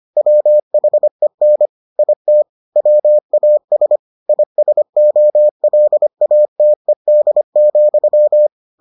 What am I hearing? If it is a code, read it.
Morse code, heard as a keyed tone, about 25 wpm, 600 Hz. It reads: WHEN IT WAS ISOLATED,